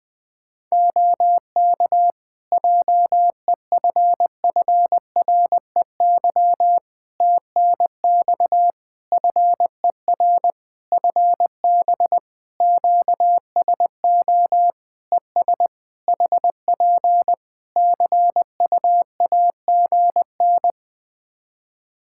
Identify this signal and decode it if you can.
Morse code, heard as a keyed tone, about 20 wpm, 700 Hz. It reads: OK JEFFREY TNX FER FB QSO ES HP CUAGN
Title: Practice QSO - Exchange 4 - Ending - 20wpm